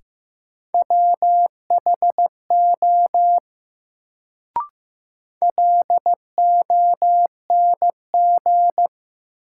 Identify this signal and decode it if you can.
Morse code, heard as a keyed tone, about 15 wpm, 700 Hz. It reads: WHO LONG
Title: Sets of 2 Words - New Word Drill - 15wpm